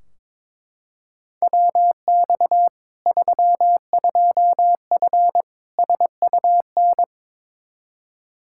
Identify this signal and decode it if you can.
Morse code, heard as a keyed tone, about 22 wpm, 700 Hz. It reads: WX 32F SUN
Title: Practice QSO - Exchange 2 - Info - 22wpm